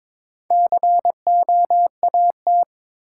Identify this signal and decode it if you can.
Morse code, heard as a keyed tone, about 22 wpm, 700 Hz. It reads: COAT
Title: Four Letter Words - 22wpm